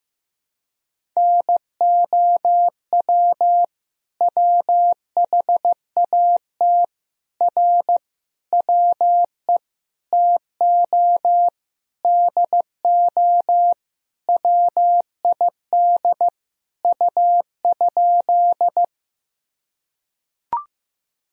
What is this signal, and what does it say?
Morse code, heard as a keyed tone, about 15 wpm, 700 Hz. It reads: NOW WHAT R WE TO DO WID U?
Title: Sentences from Top 100 Words and Abbreviations 15wpm